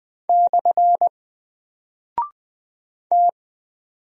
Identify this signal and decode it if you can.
Morse code, heard as a keyed tone, about 20 wpm, 700 Hz. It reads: / T
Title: Single Character (Letter:Number:Punctuation:Prosign) Alt 20wpm